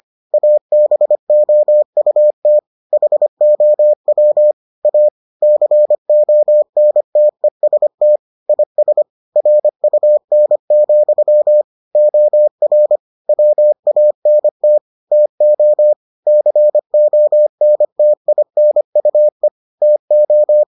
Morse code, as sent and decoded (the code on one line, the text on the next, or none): .- -... --- ..- - / .... --- .-- / .- / -.-. --- -. - . ... - / .. ... / .-. ..- -. --..-- / --- .-. / .-- .- -. - / - --- / -.-. --- -. - .. -. ..- . / - ---
ABOUT HOW A CONTEST IS RUN, OR WANT TO CONTINUE TO